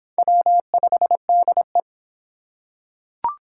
Morse code, heard as a keyed tone, about 25 wpm, 700 Hz. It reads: W5DE